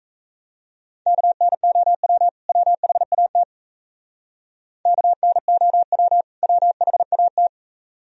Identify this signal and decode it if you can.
Morse code, heard as a keyed tone, about 36 wpm, 700 Hz. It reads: KNOW WHAT KNOW WHAT